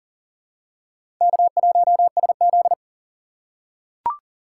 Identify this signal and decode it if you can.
Morse code, heard as a keyed tone, about 40 wpm, 700 Hz. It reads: K1SZ
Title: USA Call Signs - 6x Character Spacing - 40wpm